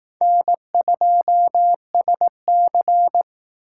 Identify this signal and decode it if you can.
Morse code, heard as a keyed tone, about 19 wpm, 700 Hz. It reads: N2SC
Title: Speed Racing - USA 1x2 Call Signs 15wpm